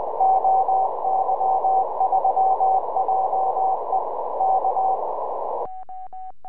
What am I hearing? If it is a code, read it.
Morse code, heard as a keyed tone, about 18 wpm, 760 Hz, with noise. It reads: OY3JE K
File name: oy3je 4m